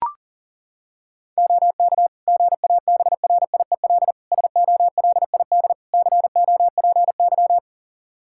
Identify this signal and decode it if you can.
Morse code, heard as a keyed tone, about 40 wpm, 700 Hz. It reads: OK GABRIEL SOLID COPY
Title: Practice QSO - Exchange 3 - More Info - 40wpm